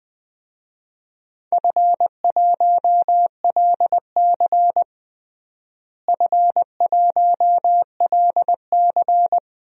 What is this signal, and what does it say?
Morse code, heard as a keyed tone, about 20 wpm, 700 Hz. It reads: F1LC F1LC
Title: DXCC-100 Call Signs - 4 Characters - Repeated 3x - 20wpm